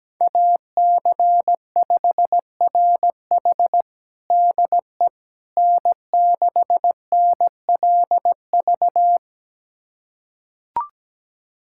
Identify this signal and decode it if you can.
Morse code, heard as a keyed tone, about 17 wpm, 700 Hz. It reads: AC5RH DE N6NLV
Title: USA Callsign DE Callsign 17wpm